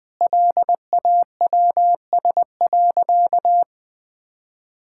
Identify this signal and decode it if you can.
Morse code, heard as a keyed tone, about 20 wpm, 700 Hz. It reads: LAWS.
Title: Sentences with 2-Word Phrases - 20wpm